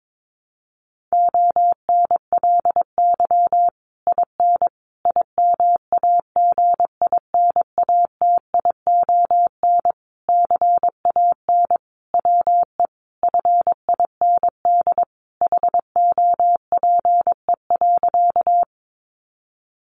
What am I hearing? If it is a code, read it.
Morse code, heard as a keyed tone, about 22 wpm, 700 Hz. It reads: ONLY IN IMAGINATION CAN WE FIND HOPE.
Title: Sentences from Top 1300 Words 22wpm